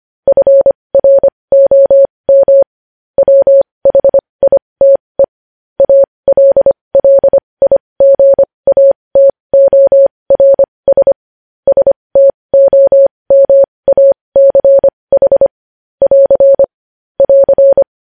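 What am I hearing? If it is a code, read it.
Morse code, heard as a keyed tone, about 25 wpm, spaced as for 20 wpm, 550 Hz. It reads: FROM WHITE ALLIGATORS STOMACH <AR> <AR>
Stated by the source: News headlines sent in Morse code at 20.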